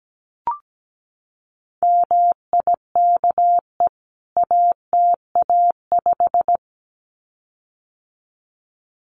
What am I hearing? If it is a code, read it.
Morse code, heard as a keyed tone, about 17 wpm, 700 Hz. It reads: MIKE ATA5
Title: CWT Exchange 17wpm